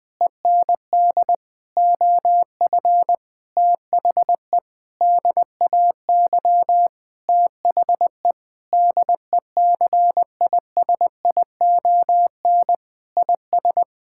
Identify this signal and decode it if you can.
Morse code, heard as a keyed tone, about 21 wpm, 700 Hz. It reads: END OF THE DAY THE DECISION IS
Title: Sentences with 8-Word Phrases - Spoken Beginning - 20wpm